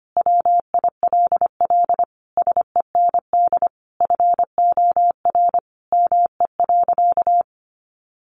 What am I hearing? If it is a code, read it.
Morse code, heard as a keyed tone, about 25 wpm, 700 Hz. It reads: WILL SEND FOR ME.
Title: Sentences from Top 1500 Words 25wpm